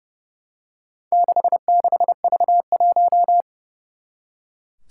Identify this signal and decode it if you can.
Morse code, heard as a keyed tone, about 30 wpm, 700 Hz. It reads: 66V1
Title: Four Challenging Letter-Number Combinations 30wpm